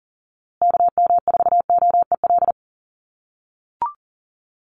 Morse code, heard as a keyed tone, about 40 wpm, 700 Hz. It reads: KM4OEL
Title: USA Call Signs Repeated 2x 40wpm